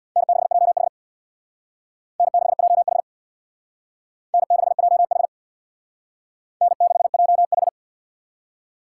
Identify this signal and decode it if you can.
Morse code, heard as a keyed tone, about 61 wpm, 700 Hz. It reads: N6JH N6JH N6JH N6JH
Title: Speed Racing - USA 1x2 Call Signs 50wpm